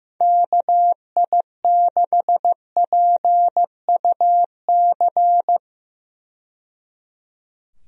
Morse code, heard as a keyed tone, about 15 wpm, 700 Hz. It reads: KI6PUC
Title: USA Call Signs - 6 Characters - 15wpm